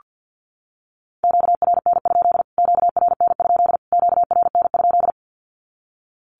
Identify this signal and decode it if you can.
Morse code, heard as a keyed tone, about 48 wpm, 700 Hz. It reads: QRN? QRN? QRN?
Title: Q-Codes Repeated 3x 50wpm